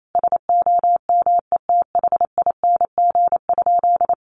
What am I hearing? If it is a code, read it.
Morse code, heard as a keyed tone, about 28 wpm, 700 Hz. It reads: SOMETHING?
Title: Sentences from Top 1100 Words 28wpm